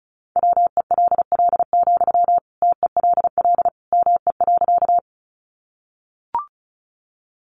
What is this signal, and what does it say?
Morse code, heard as a keyed tone, about 34 wpm, 700 Hz. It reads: WELL, TELL ME.
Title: Sentences from Top 200 Words 35wpm